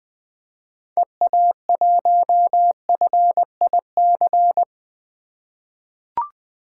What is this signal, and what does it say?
Morse code, heard as a keyed tone, about 20 wpm, 700 Hz. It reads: EA1FIC
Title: 2023 POTA Hunter Call Signs - 20wpm